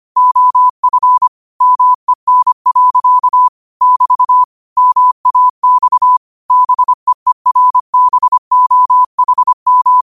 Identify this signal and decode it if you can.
Morse code, heard as a keyed tone, about 25 wpm, 1000 Hz. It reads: OF MEN. <BT> MAX BEERBOHM
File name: qotdfiles / 25wpm